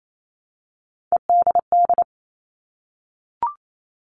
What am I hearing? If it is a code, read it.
Morse code, heard as a keyed tone, about 28 wpm, 700 Hz. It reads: EDD
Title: All Cut Numbers - 3-Digit 28wpm